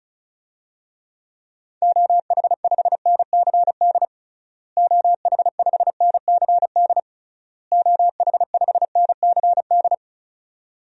Morse code, heard as a keyed tone, about 36 wpm, 700 Hz. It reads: OH5NCD OH5NCD OH5NCD
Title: DXCC-100 Call Signs - 6 Characters - Repeated 3x - 35wpm